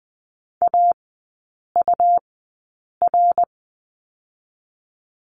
Morse code, heard as a keyed tone, about 20 wpm, spaced as for 8 wpm, 700 Hz. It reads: AUR
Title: 100 Lesson 17e - Letter u - Three Characters 20wpm